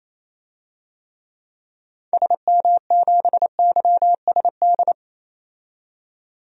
Morse code, heard as a keyed tone, about 28 wpm, 700 Hz. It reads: SM7YSD
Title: DXCC-100 Call Signs - 6 Characters - 28wpm